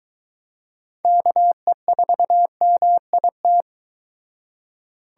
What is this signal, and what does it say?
Morse code, heard as a keyed tone, about 23 wpm, 700 Hz. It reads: KE4MIT